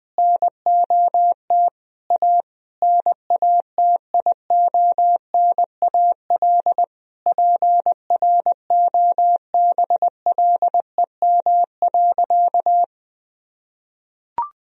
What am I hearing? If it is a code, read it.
Morse code, heard as a keyed tone, about 21 wpm, 700 Hz. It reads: NOT A NATIONAL PROBLEM.
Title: Sentences from Top 1000 Words - Review Part 4 - 20wpm